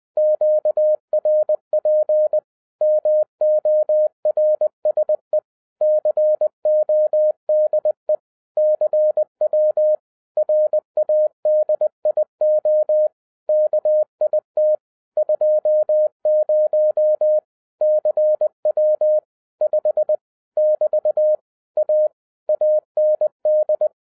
Morse code, heard as a keyed tone, about 20 wpm, 610 Hz. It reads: QRP MORSE CODE CW RADIO KIT 20 CW 5 <BT> A AND
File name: 2011-07-mp3 / n23 QRP OTAKU 20wpm